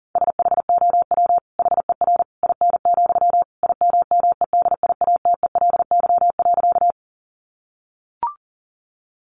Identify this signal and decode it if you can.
Morse code, heard as a keyed tone, about 40 wpm, 700 Hz. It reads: SHOW HER IN, IMMEDIATELY.
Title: Sentences from Top 1100 Words 40wpm